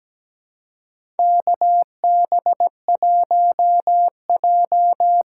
Morse code, heard as a keyed tone, about 17 wpm, 700 Hz. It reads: KB1J
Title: USA Call Signs - 6 Characters - 6x Character Spacing - 17wpm